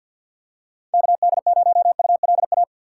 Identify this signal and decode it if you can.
Morse code, heard as a keyed tone, about 53 wpm, 700 Hz. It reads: KD0ULA
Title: USA Call Signs - 3x Character Spacing - 50wpm